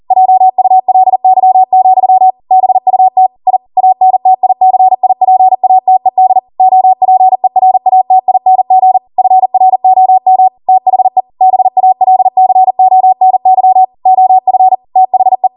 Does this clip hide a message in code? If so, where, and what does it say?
Yes: Morse code from 0.1 to 15.5 s, heard as a keyed tone, about 40 wpm, 750 Hz. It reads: JULY, BUT I ANTICIPATED OPERATING FROM THE BALCONY OF THE